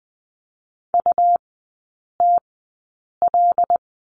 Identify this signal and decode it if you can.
Morse code, heard as a keyed tone, about 20 wpm, spaced as for 8 wpm, 690 Hz. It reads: UTL
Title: 145 Lesson 24e - Letter y - Three Characters 20wpm